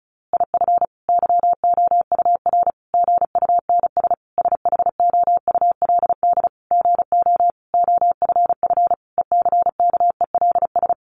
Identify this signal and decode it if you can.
Morse code, heard as a keyed tone, about 34 wpm, 700 Hz. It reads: IF YOUR GUNS SHOULD GO OFF ECKELS
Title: A Sound Of Thunder 35wpm